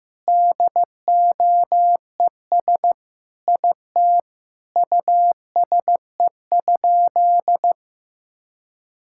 Morse code, heard as a keyed tone, about 15 wpm, 700 Hz. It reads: DOES IT USE?
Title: Sentences from Top 1000 Words - Review Part 6 - 15wpm